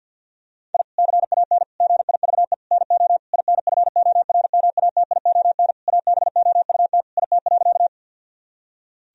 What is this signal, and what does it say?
Morse code, heard as a keyed tone, about 49 wpm, 700 Hz. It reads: I CAN GIVE NO INFORMATION ABOUT IT.
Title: Sentences from Top 1000 Words - Review Part 3 - 50wpm